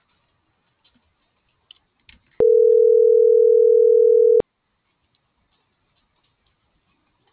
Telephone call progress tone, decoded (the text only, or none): ringback tone